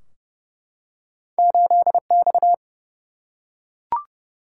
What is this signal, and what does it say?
Morse code, heard as a keyed tone, about 31 wpm, 700 Hz. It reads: 8X
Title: Single-Letter Single-Number 30wpm